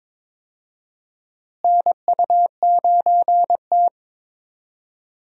Morse code, heard as a keyed tone, about 21 wpm, 700 Hz. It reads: NU9T
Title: USA Call Signs - 4 Characters - Repeated 2x - 22wpm